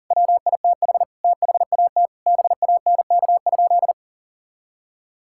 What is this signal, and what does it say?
Morse code, heard as a keyed tone, about 40 wpm, 700 Hz. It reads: WITH THAT BANK?
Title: Sentences from Top 1000 Words - Review Part 6 - 40wpm